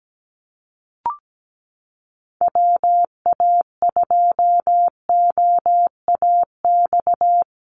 Morse code, heard as a keyed tone, about 17 wpm, 700 Hz. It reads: WA2OAX
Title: Call Signs Repeated 2x - 17wpm